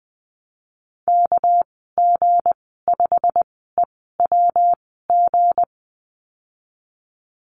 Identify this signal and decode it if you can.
Morse code, heard as a keyed tone, about 20 wpm, spaced as for 15 wpm, 700 Hz. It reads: KG5EWG
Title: 200 Lesson 32g - Letter k - Call Signs 20wpm